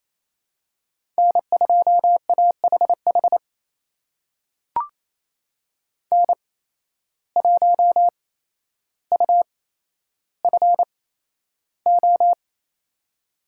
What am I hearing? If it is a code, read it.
Morse code, heard as a keyed tone, about 28 wpm, spaced as for 12 wpm, 700 Hz. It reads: N2AHH N1UFO
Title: USA Call Signs - 5 Characters - 8x Character Spacing - 28wpm